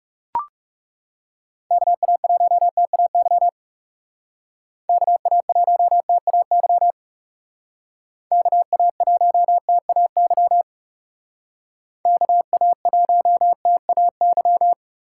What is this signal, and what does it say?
Morse code, heard as a keyed tone, about 38 wpm, 700 Hz. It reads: KA1TAY KA1TAY KA1TAY KA1TAY